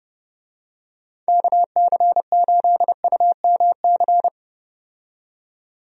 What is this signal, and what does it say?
Morse code, heard as a keyed tone, about 30 wpm, 700 Hz. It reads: KC8UMC
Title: USA Call Signs - 6 Characters - 30wpm